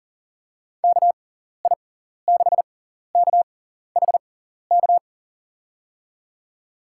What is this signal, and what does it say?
Morse code, heard as a keyed tone, about 39 wpm, spaced as for 14 wpm, 700 Hz. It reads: KI6KHK
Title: USA Call Signs - 6 Characters - 6x Character Spacing - 40wpm